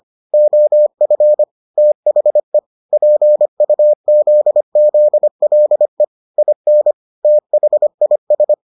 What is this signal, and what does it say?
Morse code, heard as a keyed tone, about 25 wpm, 600 Hz. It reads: OF THE PUZZLE IN THIS